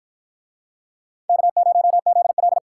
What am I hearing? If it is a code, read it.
Morse code, heard as a keyed tone, about 53 wpm, 700 Hz. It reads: K0ZL